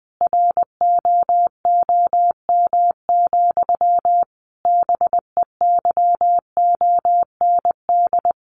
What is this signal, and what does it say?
Morse code, heard as a keyed tone, about 20 wpm, 700 Hz. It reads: ROOM, BEYOND
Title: A Sound Of Thunder 20wpm